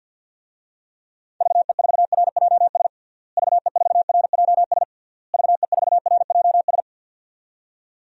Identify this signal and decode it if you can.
Morse code, heard as a keyed tone, about 51 wpm, 700 Hz. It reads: VE4RJS VE4RJS VE4RJS
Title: DXCC-100 Call Signs - 6 Characters - Repeated 3x - 50wpm